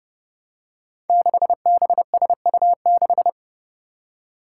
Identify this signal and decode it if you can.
Morse code, heard as a keyed tone, about 30 wpm, 700 Hz. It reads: 6BSU6
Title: Five Challenging Letter-Number Combinations 30wpm